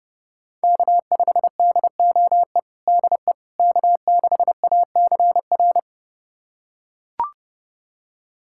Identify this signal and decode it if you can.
Morse code, heard as a keyed tone, about 29 wpm, 700 Hz. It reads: K5DOE DE K6ACR
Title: USA Callsign DE Callsign - 5 Characters - 30wpm